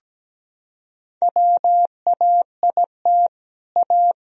Morse code, heard as a keyed tone, about 17 wpm, 700 Hz. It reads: WAIT A
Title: Sentences from Top 1000 Words - Review Part 4 - 17wpm